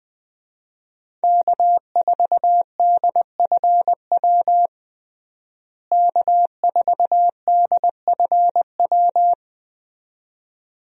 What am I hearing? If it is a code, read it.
Morse code, heard as a keyed tone, about 21 wpm, 700 Hz. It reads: K4DFW K4DFW
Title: USA Call Signs - 5 Characters - Repeated 2x - 20wpm